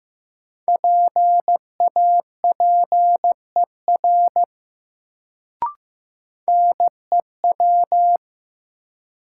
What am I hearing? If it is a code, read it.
Morse code, heard as a keyed tone, about 16 wpm, 700 Hz. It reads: PAPER NEW
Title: Sentences from Top 300 Words - New Word Drill - 15wpm